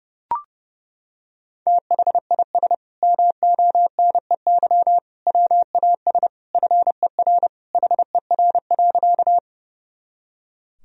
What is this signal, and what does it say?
Morse code, heard as a keyed tone, about 31 wpm, 700 Hz. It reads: THIS MONEY WAS FER HER.
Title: Sentences from Top 400 Words and Abbreviations 30wpm